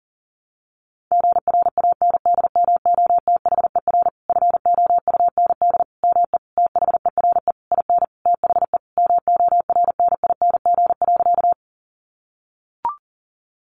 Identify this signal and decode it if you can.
Morse code, heard as a keyed tone, about 40 wpm, 700 Hz. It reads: GRANDMOTHER FOUND ME THERE IN THE MORNING.
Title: Sentences from Top 1200 Words 40wpm